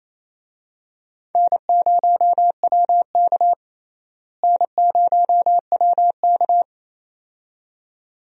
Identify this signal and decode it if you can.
Morse code, heard as a keyed tone, about 28 wpm, 700 Hz. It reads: N0WK N0WK
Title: USA Call Signs - 4 Characters - Repeated 2x - 28wpm